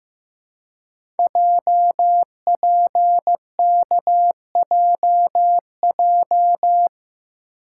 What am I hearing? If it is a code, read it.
Morse code, heard as a keyed tone, about 15 wpm, 700 Hz. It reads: JPKJJ
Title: Code Groups - 5 Letter:Numbers - 15wpm